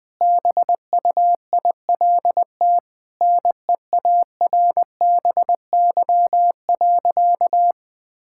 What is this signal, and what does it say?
Morse code, heard as a keyed tone, about 20 wpm, 700 Hz. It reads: BUILT NEARBY.
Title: Sentences with 2-Word Phrases - Spoken Beginning - 20wpm